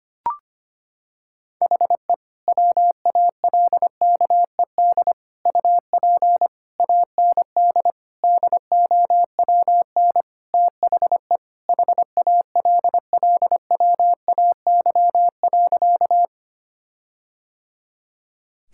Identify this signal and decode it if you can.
Morse code, heard as a keyed tone, about 25 wpm, 700 Hz. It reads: HE WALKED UP AND DOWN THE HALLWAY.
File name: practice / 25wpm / Sentences with 4-Word Phrases - 25wpm